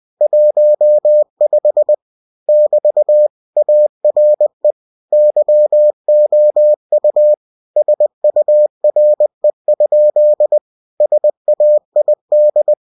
Morse code, heard as a keyed tone, about 20 wpm, 600 Hz. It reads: 15 <BT> ARE YOU SURE? SAID
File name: cw-books / winnie-the-pooh / winnie-20wpm-page-0015